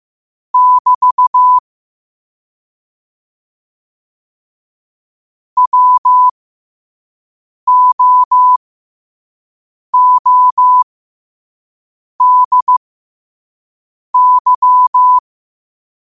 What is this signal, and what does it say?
Morse code, heard as a keyed tone, about 15 wpm, spaced as for 5 wpm, 1000 Hz. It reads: <BT> WOODY
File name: qotdfiles / 5wpm